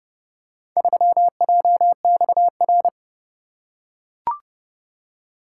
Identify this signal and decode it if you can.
Morse code, heard as a keyed tone, about 29 wpm, 700 Hz. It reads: 3JXR